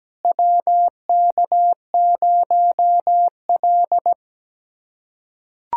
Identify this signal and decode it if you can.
Morse code, heard as a keyed tone, about 17 wpm, 700 Hz. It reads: WK0L
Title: USA Callsign DE Callsign - 4 Characters - 17wpm